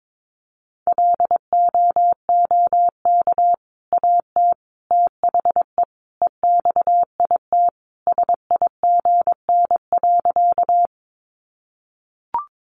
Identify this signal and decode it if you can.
Morse code, heard as a keyed tone, about 22 wpm, 700 Hz. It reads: LOOK AT THE EXIT SIGN.
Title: Sentences from Top 1100 Words 22wpm